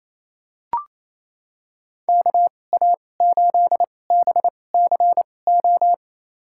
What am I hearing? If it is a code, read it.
Morse code, heard as a keyed tone, about 28 wpm, spaced as for 20 wpm, 700 Hz. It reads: KA8BCO
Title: USA Call Signs - 2x Character Spacing - 28wpm